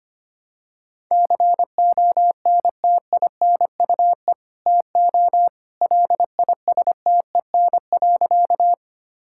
Morse code, heard as a keyed tone, about 25 wpm, 700 Hz. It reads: CONTINUE TO LISTEN.
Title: Sentences from Top 1000 Words - Review Part 6 - 25wpm